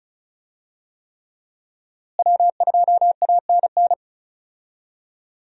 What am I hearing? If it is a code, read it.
Morse code, heard as a keyed tone, about 34 wpm, 700 Hz. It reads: W2ANN
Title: USA Call Signs - 5 Characters - Repeated 3x - 35wpm